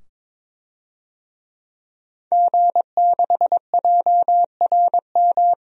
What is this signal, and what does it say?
Morse code, heard as a keyed tone, about 21 wpm, 700 Hz. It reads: G6JRM
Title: DXCC-100 Call Signs - 5 Characters - 22wpm